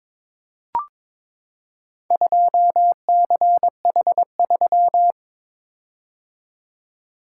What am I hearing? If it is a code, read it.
Morse code, heard as a keyed tone, about 23 wpm, 700 Hz. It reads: 2CH3
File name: Four Letter-Number Combinations - 22wpm